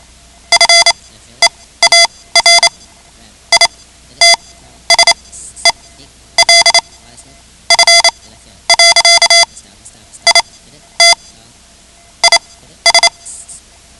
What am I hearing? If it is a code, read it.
Morse code, heard as a keyed tone, about 28 wpm, spaced as for 12 wpm, 2050 Hz. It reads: FEAR ITSELF. IT IS
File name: ham radio / CodeQuick / Code Quick 10